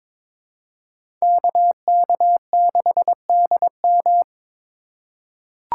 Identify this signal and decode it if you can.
Morse code, heard as a keyed tone, about 22 wpm, 700 Hz. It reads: KK6DM
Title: USA Call Signs - 4x Character Spacing - 22wpm